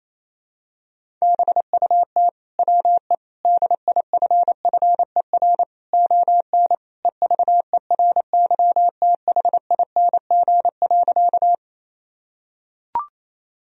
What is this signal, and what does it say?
Morse code, heard as a keyed tone, about 28 wpm, 700 Hz. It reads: BUT WE DIFFER ON EVERYTHING.
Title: Sentences from Top 1000 Words - Review Part 1 - 28wpm